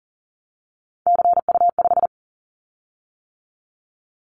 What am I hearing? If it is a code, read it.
Morse code, heard as a keyed tone, about 39 wpm, 700 Hz. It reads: CU5
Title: Callsign Prefixes for Top 50 DXCC - 40wpm